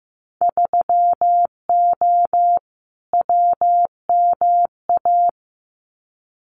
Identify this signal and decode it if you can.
Morse code, heard as a keyed tone, about 15 wpm, 700 Hz. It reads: 3O WMA
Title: Winter Field Day Exchanges Encore 15wpm